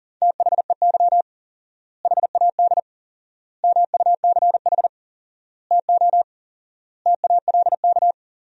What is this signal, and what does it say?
Morse code, heard as a keyed tone, about 40 wpm, 700 Hz. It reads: THEY HAD MUCH TO TALK
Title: 4x Word-Spacing - Sentences from Top 400 Words - 40wpm